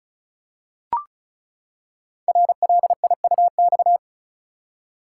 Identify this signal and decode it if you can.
Morse code, heard as a keyed tone, about 36 wpm, 700 Hz. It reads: RLIUX
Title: Code Groups - 5 Letter:Numbers - 35wpm